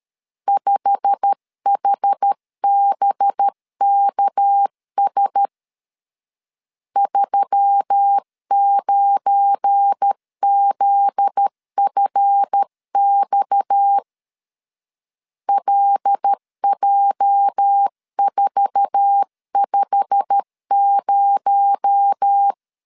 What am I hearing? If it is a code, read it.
Morse code, heard as a keyed tone, about 13 wpm, 800 Hz. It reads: 5HBKS 39ZFX LJ450
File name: LEZIONI / 8 YJ45 / 8 YJ45 12wpm 2151